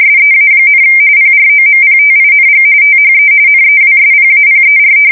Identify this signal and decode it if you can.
RTTY, about 45.45 baud. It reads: SEA, PROMPTING THE NEWSPAPER
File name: fsk170 / battleshipnj